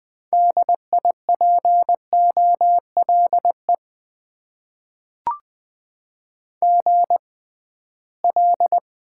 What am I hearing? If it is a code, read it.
Morse code, heard as a keyed tone, about 20 wpm, spaced as for 15 wpm, 700 Hz. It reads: DIPOLE GL
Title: Top 90 QSO Elements - 6x Character Spacing - 20wpm